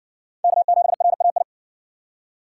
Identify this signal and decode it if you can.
Morse code, heard as a keyed tone, about 56 wpm, 700 Hz. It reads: K7RAI